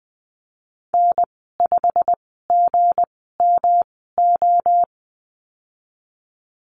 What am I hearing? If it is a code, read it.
Morse code, heard as a keyed tone, about 20 wpm, spaced as for 15 wpm, 700 Hz. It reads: N5GMO
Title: 207 Lesson 33g - Letter j - Call Signs 20wpm